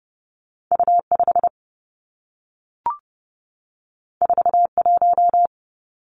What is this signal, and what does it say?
Morse code, heard as a keyed tone, about 30 wpm, 700 Hz. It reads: U5 41
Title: Two Challenging Letter-Number Combinations 30wpm